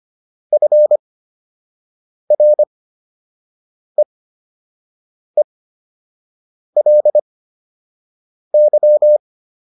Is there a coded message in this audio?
Yes: Morse code from 0.5 to 9.2 s, heard as a keyed tone, about 25 wpm, spaced as for 6 wpm, 600 Hz. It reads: FREELY